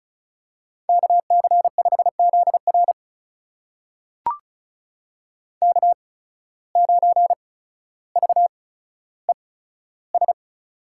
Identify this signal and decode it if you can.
Morse code, heard as a keyed tone, about 34 wpm, spaced as for 15 wpm, 700 Hz. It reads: KC5ZR K9VES
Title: USA Call Signs - 5 Characters - 8x Character Spacing - 35wpm